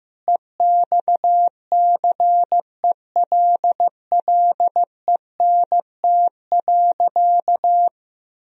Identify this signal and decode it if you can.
Morse code, heard as a keyed tone, about 15 wpm, 700 Hz. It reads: EXCELLENT.
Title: Sentences from Top 1000 Words - Review Part 4 - 15wpm